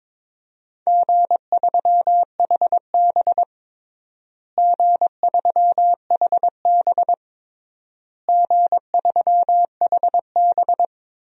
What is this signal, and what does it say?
Morse code, heard as a keyed tone, about 22 wpm, 700 Hz. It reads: G3HB G3HB G3HB
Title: DXCC-100 Call Signs - 4 Characters - Repeated 3x - 22wpm